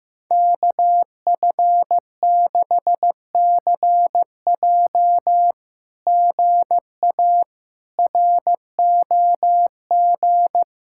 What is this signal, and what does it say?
Morse code, heard as a keyed tone, about 15 wpm, 700 Hz. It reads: KF6CJ GA ROG
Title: SOTA QSO Practice 15wpm